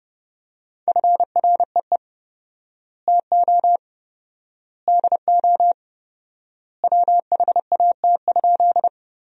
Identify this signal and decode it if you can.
Morse code, heard as a keyed tone, about 31 wpm, 700 Hz. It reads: FREE TO DO WHAT?
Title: 4x Word-Spacing - Sentences from Top 500 Words - 30wpm